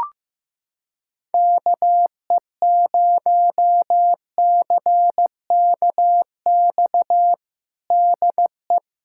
Morse code, heard as a keyed tone, about 15 wpm, 700 Hz. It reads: KE0CKX DE
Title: USA Callsign DE Callsign - 6 Characters - 15wpm